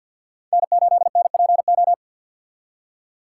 Morse code, heard as a keyed tone, about 62 wpm, 700 Hz. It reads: N8NPO
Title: Speed Racing - USA 1x3 Call Signs 50wpm